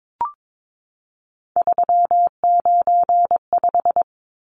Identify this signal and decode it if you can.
Morse code, heard as a keyed tone, about 22 wpm, 700 Hz. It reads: 395
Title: Numbers - 3-Digit 22wpm